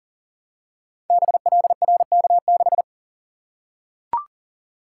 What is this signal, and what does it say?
Morse code, heard as a keyed tone, about 39 wpm, 700 Hz. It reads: BLRK6
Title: Code Groups - 5 Letter:Numbers - 40wpm